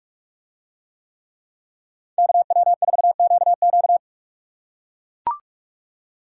Morse code, heard as a keyed tone, about 43 wpm, 700 Hz. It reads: KW4QQ